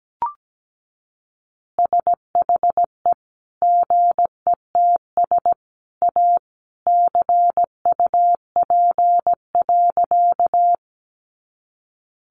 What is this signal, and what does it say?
Morse code, heard as a keyed tone, about 17 wpm, 700 Hz. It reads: SHE GETS A CUP.
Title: Additional Easy Phrases 17wpm